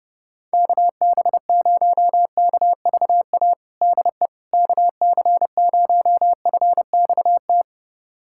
Morse code, heard as a keyed tone, about 30 wpm, 700 Hz. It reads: KB0KVA DE KC0FXT
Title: USA Callsign DE Callsign - 6 Characters - 30wpm